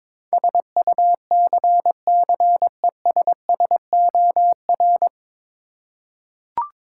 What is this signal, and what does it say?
Morse code, heard as a keyed tone, about 22 wpm, 700 Hz. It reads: SUCCESSOR
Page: Morse Code Ninja practice 22wpm